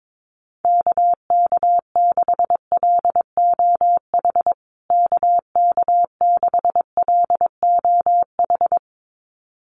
Morse code, heard as a keyed tone, about 22 wpm, 700 Hz. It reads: KK6LOH KK6LOH
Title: USA Call Signs Repeated 2x 22wpm